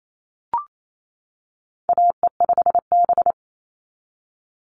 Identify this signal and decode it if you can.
Morse code, heard as a keyed tone, about 28 wpm, 700 Hz. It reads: AE5B
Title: NA SOTA Chaser Callsigns 28wpm